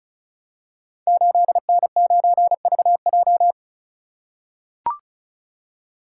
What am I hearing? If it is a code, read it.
Morse code, heard as a keyed tone, about 36 wpm, 690 Hz. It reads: 8N9VJ
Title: Code Groups - 5 Letter:Numbers - 35wpm